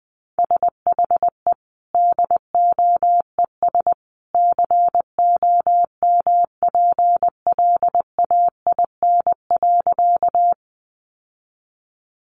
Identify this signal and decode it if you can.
Morse code, heard as a keyed tone, about 20 wpm, 700 Hz. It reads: SHE DOES COMPLAIN.
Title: Sentences from Top 1300 Words 20wpm